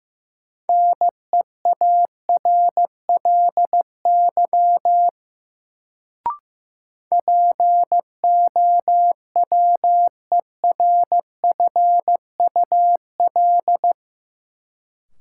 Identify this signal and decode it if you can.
Morse code, heard as a keyed tone, about 15 wpm, 700 Hz. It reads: NEARLY POWERFUL
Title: Sentences from Top 1300 Words - New Word Drill Alt - 15wpm